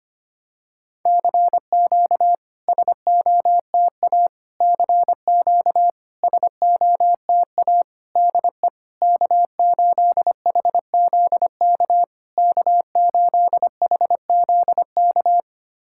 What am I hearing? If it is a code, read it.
Morse code, heard as a keyed tone, about 25 wpm, 660 Hz. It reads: CQ SOTA CQ SOTA DE K8HZK K8HZK
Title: SOTA QSO Practice 25wpm